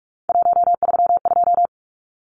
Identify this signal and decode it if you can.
Morse code, heard as a keyed tone, about 44 wpm, 700 Hz. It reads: 132
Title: Numbers - 3-Digit 45wpm